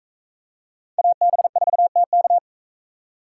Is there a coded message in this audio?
Yes: Morse code from 1.0 to 2.4 s, heard as a keyed tone, about 41 wpm, 700 Hz. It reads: AB4TK